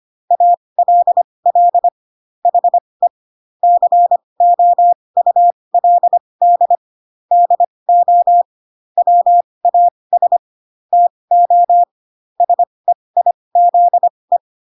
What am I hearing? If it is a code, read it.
Morse code, heard as a keyed tone, about 25 wpm, spaced as for 20 wpm, 700 Hz. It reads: ALL HE COULD DO WAS TO SEIZE